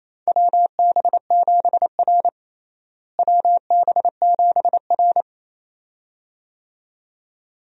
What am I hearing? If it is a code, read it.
Morse code, heard as a keyed tone, about 28 wpm, 700 Hz. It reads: WB7R WB7R
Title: USA Call Signs - 4 Characters - Repeated 2x - 28wpm